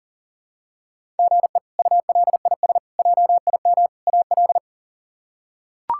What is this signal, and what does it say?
Morse code, heard as a keyed tone, about 40 wpm, 700 Hz. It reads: GE ULIS JIM AL
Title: K1USN SST QSO Practice - 40wpm